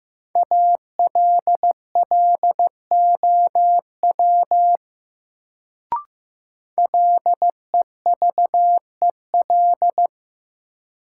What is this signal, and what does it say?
Morse code, heard as a keyed tone, about 16 wpm, 700 Hz. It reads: ALLOW LEVEL
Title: Sentences from Top 700 Words - New Word Drill - 15wpm